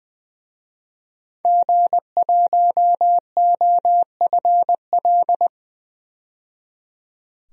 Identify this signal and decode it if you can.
Morse code, heard as a keyed tone, about 20 wpm, 700 Hz. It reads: G1OFL
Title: DXCC-100 Call Signs - 5 Characters - 20wpm